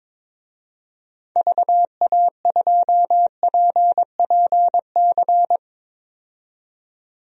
Morse code, heard as a keyed tone, about 22 wpm, 700 Hz. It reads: VA2PPC
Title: DXCC-100 Call Signs - 6 Characters - 22wpm